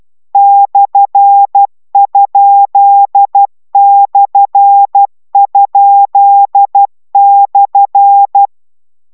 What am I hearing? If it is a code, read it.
Morse code, heard as a keyed tone, about 12 wpm, 800 Hz. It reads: /?/?/